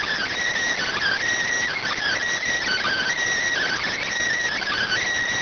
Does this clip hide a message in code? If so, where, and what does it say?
Yes: Emergency Alert System (SAME) tones (header bursts) from 0.8 to 5.0 s, heard as harsh buzzing bursts.